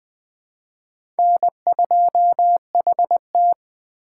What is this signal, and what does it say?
Morse code, heard as a keyed tone, about 20 wpm, 700 Hz. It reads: N2HT
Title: USA Call Signs - 4 Characters - 3x Character Spacing - 20wpm